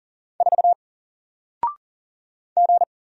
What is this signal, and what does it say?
Morse code, heard as a keyed tone, about 38 wpm, 700 Hz. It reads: 4 G
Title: Single Letter-Number 40wpm